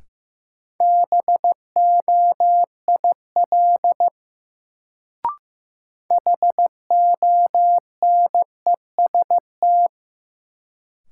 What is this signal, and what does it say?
Morse code, heard as a keyed tone, about 15 wpm, 700 Hz. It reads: BOIL HONEST
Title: Sets of 5 Words Encore - New Word Drill - 15wpm